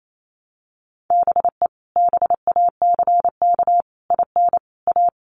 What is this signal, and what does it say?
Morse code, heard as a keyed tone, about 28 wpm, 700 Hz. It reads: BE BACK IN A
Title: Sets of 5 Words 28wpm